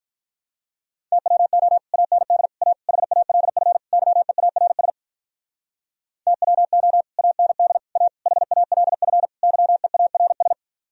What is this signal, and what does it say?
Morse code, heard as a keyed tone, about 43 wpm, 700 Hz. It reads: TWO AND A HALF YEARS TWO AND A HALF YEARS
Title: Speed-Racing - Sets of 5 Words 35wpm